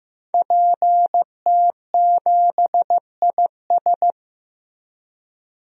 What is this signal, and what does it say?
Morse code, heard as a keyed tone, about 15 wpm, 700 Hz. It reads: PT7IS
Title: DXCC-100 Call Signs - 5 Characters - Repeated 2x - 15wpm